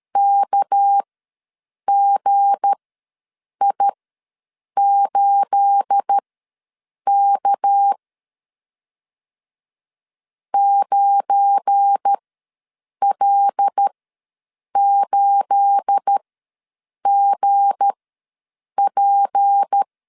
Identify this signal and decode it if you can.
Morse code, heard as a keyed tone, about 13 wpm, spaced as for 7 wpm, 800 Hz. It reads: KGI8K 9L8GP
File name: LEZIONI / 9 28 ws 12wpm 3354